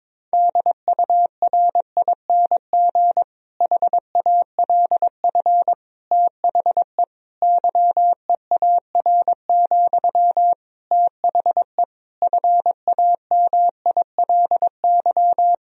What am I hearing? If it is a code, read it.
Morse code, heard as a keyed tone, about 22 wpm, 700 Hz. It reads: DURING HALF THE YEAR, THE FAMILY
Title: Sentences from Top 1000 Words - Review Part 2 - 22wpm